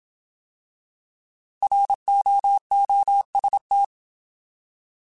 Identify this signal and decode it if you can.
Morse code, heard as a keyed tone, about 27 wpm, 800 Hz. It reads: ROOST
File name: Alphabetical / Alphabetical mp3s / Rs @25